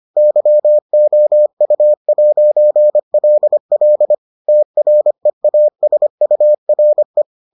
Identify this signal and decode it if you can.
Morse code, heard as a keyed tone, about 25 wpm, 600 Hz. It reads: YOU'LL TREASURE